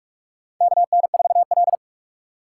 Morse code, heard as a keyed tone, about 46 wpm, 700 Hz. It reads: KN4L
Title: USA Call Signs - 4 Characters - 4x Character Spacing - 45wpm